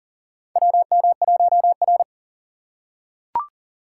Morse code, heard as a keyed tone, about 40 wpm, 700 Hz. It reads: WM1R
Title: USA Call Signs - 4 Characters - 2x Character Spacing - 40wpm